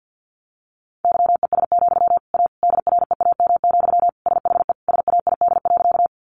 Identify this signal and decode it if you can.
Morse code, heard as a keyed tone, about 50 wpm, 700 Hz. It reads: YES, A DREAM, SHE SAID.
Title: Sentences from Top 1500 Words 50wpm